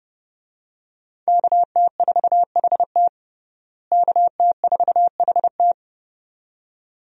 Morse code, heard as a keyed tone, about 32 wpm, 700 Hz. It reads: KT4HT KT4HT
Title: USA Call Signs - 5 Characters - Repeated 2x - 30wpm